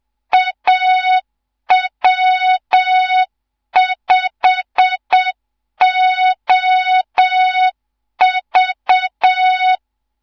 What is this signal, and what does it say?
Morse code, heard as a keyed tone, about 7 wpm, 3700 Hz, with noise. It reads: AW5OV
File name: Morze-HA2MN / hang / 14 280 HA2MN